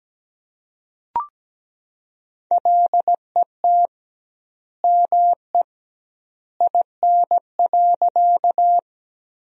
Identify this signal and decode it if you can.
Morse code, heard as a keyed tone, about 17 wpm, 700 Hz. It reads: LET ME IN.
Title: 2x Word-Spacing - Sentences from Top 300 Words - 17wpm